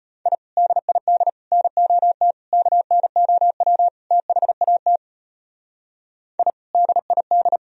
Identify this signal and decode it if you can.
Morse code, heard as a keyed tone, about 31 wpm, 700 Hz. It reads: I DID NOT KNOW THAT I DID
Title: Speed-Racing - Sets of 5 Words 25wpm